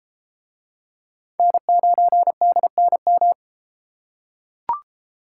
Morse code, heard as a keyed tone, about 33 wpm, 700 Hz. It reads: N9DNM